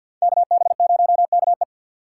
Morse code, heard as a keyed tone, about 52 wpm, 700 Hz. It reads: KB0KE
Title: USA Call Signs - 5 Characters - 2x Character Spacing - 50wpm